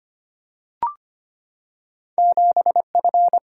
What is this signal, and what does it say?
Morse code, heard as a keyed tone, about 25 wpm, 700 Hz. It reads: 7F
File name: Two Letter-Number Combinations - 25wpm